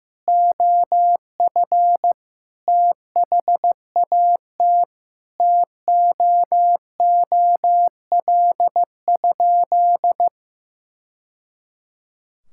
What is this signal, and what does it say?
Morse code, heard as a keyed tone, about 15 wpm, 700 Hz. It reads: OF THAT TOOL?
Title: Sentences from Top 1000 Words - Review Part 6 - 15wpm